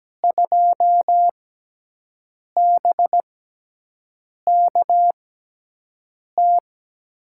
Morse code, heard as a keyed tone, about 17 wpm, spaced as for 6 wpm, 700 Hz. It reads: 2BKT
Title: USA Call Signs - 5 Characters - 6x Character Spacing - 17wpm